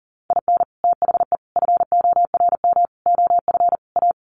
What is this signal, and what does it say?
Morse code, heard as a keyed tone, about 40 wpm, 700 Hz. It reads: IN THE FORM OF A
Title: Sets of 5 Words 40wpm